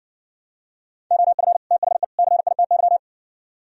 Morse code, heard as a keyed tone, about 67 wpm, 700 Hz. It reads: OF THE CITY